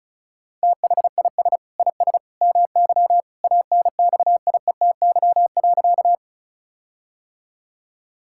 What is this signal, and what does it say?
Morse code, heard as a keyed tone, about 34 wpm, 700 Hz. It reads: THIS IS MY ANXIETY.
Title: Sentences from Top 1600 Words 35wpm